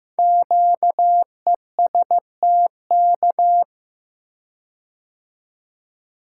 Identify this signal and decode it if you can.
Morse code, heard as a keyed tone, about 15 wpm, 700 Hz. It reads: QESTK
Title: Code Groups - 5 Letters - 15wpm